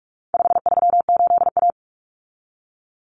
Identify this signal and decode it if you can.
Morse code, heard as a keyed tone, about 44 wpm, 700 Hz. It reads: 538A
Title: Common Cut Numbers - 4-Digit 45wpm